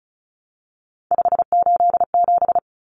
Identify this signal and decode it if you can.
Morse code, heard as a keyed tone, about 34 wpm, 700 Hz. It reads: 587
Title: Numbers - 3-Digit 35wpm